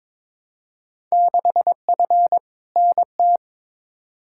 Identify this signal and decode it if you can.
Morse code, heard as a keyed tone, about 22 wpm, 700 Hz. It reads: 6F NT
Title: Field Day Exchanges Encore - 22wpm